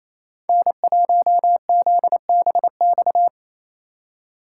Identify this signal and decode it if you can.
Morse code, heard as a keyed tone, about 34 wpm, 700 Hz. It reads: N1ZBX
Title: Speed Racing - USA 1x3 Call Signs 28wpm